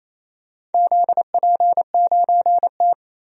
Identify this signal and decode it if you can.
Morse code, heard as a keyed tone, about 28 wpm, 700 Hz. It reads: ZP9T
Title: DXCC-100 Call Signs - 4 Characters - 28wpm